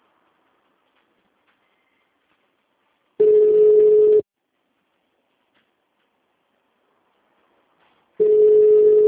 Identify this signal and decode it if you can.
Ringback tone (Europe, 425 Hz).